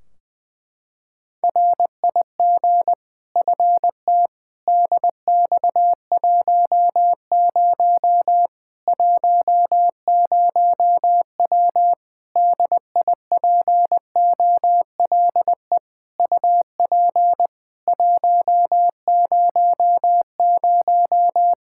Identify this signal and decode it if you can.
Morse code, heard as a keyed tone, about 20 wpm, 700 Hz. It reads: RIG FT DX10 10W DIPOLE UP 100
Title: Practice QSO - Exchange 2 - Info - 20wpm